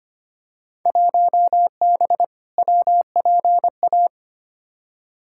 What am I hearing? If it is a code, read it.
Morse code, heard as a keyed tone, about 25 wpm, 700 Hz. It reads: 1B WPA
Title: Field Day Exchanges Encore - 25wpm